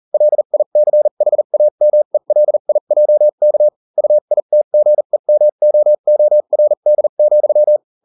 Morse code, heard as a keyed tone, about 40 wpm, spaced as for 35 wpm, 600 Hz. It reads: LICHAMELIJK UITGEMOORD,